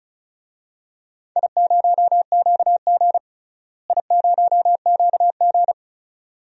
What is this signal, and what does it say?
Morse code, heard as a keyed tone, about 34 wpm, 700 Hz. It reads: I0QG I0QG
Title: DXCC-100 Call Signs - 4 Characters - Repeated 2x - 35wpm